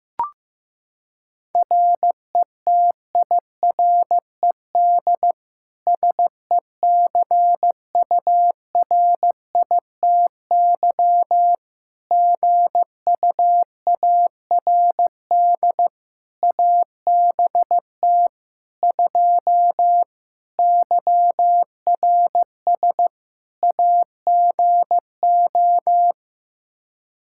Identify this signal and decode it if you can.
Morse code, heard as a keyed tone, about 15 wpm, 700 Hz. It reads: RETIRED SECURITY GUARD ABT 2 YRS AGO
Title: Practice QSO - Exchange 3 - More Info - 15wpm